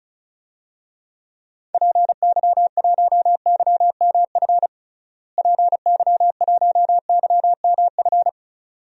Morse code, heard as a keyed tone, about 34 wpm, 700 Hz. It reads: PY1YMF PY1YMF
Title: DXCC-100 Call Signs - 6 Characters - Repeated 2x - 35wpm